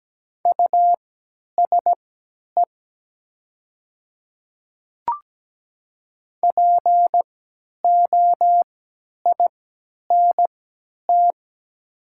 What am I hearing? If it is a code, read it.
Morse code, heard as a keyed tone, about 17 wpm, spaced as for 10 wpm, 700 Hz. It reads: USE POINT
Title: 3x Character Spacing - Top 200 Words - 17wpm